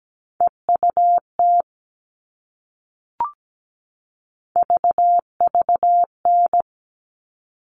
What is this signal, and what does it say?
Morse code, heard as a keyed tone, about 17 wpm, 700 Hz. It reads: EUT VVN
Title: All Cut Numbers - 3-Digit 17wpm